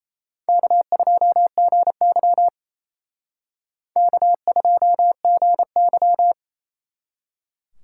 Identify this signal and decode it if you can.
Morse code, heard as a keyed tone, about 34 wpm, 700 Hz. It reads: K2GY K2GY
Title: Speed Racing - USA 1x2 Call Signs 28wpm